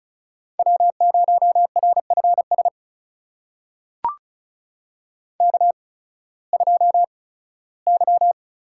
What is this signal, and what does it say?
Morse code, heard as a keyed tone, about 36 wpm, spaced as for 19 wpm, 700 Hz. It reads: W0RFS K2Y
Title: USA Call Signs - 5 Characters - 8x Character Spacing - 35wpm